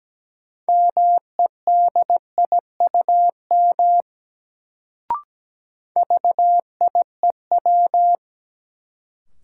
Morse code, heard as a keyed tone, about 17 wpm, 700 Hz. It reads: MEDIUM VIEW
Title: Sentences from Top 900 Words - New Word Drill - 17wpm